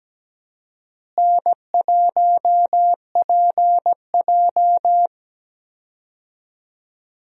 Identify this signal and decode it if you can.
Morse code, heard as a keyed tone, about 17 wpm, 700 Hz. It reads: N1PJ
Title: USA Call Signs - 4 Characters - Repeated 2x - 17wpm